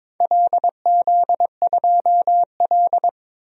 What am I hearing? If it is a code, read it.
Morse code, heard as a keyed tone, about 22 wpm, 700 Hz. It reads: LZ2L
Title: DXCC-100 Call Signs - 4 Characters - 22wpm